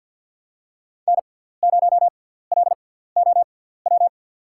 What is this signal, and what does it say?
Morse code, heard as a keyed tone, about 52 wpm, spaced as for 17 wpm, 690 Hz. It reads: N0LOW
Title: USA Call Signs - 5 Characters - 6x Character Spacing - 50wpm